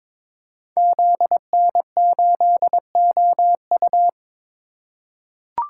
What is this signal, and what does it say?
Morse code, heard as a keyed tone, about 22 wpm, 700 Hz. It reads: ZN8OU
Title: Code Groups - 5 Letter:Numbers - 22wpm